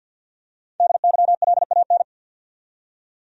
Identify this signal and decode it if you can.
Morse code, heard as a keyed tone, about 51 wpm, 700 Hz. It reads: DYLAN
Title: USA Common Names - 50wpm